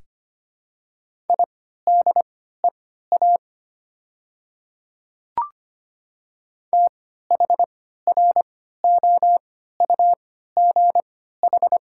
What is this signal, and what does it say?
Morse code, heard as a keyed tone, about 25 wpm, spaced as for 14 wpm, 700 Hz. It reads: IDEA THROUGH
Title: 3x Character Spacing - Top 300 Words - 25wpm